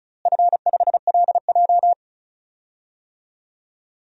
Morse code, heard as a keyed tone, about 36 wpm, 700 Hz. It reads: F5LJ
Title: DXCC-100 Call Signs - 4 Characters - Repeated 2x - 35wpm